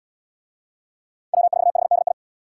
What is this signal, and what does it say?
Morse code, heard as a keyed tone, about 70 wpm, 690 Hz. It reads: W4HDI